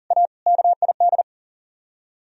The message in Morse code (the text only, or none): A KID